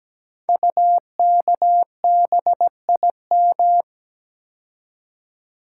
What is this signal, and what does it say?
Morse code, heard as a keyed tone, about 17 wpm, 700 Hz. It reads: UKBIM
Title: Code Groups - 5 Letters - 17wpm